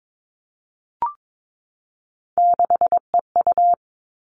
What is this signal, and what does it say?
Morse code, heard as a keyed tone, about 22 wpm, 700 Hz. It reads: 6EU
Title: All Cut Numbers - 3-Digit 22wpm